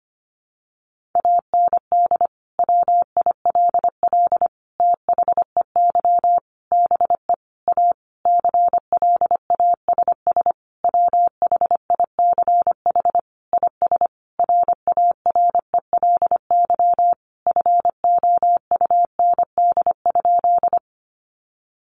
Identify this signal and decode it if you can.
Morse code, heard as a keyed tone, about 25 wpm, 700 Hz. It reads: AND WILL THEY BE A CLASS WHICH IS RARELY FOUND?
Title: Sentences from Top 1400 Words 25wpm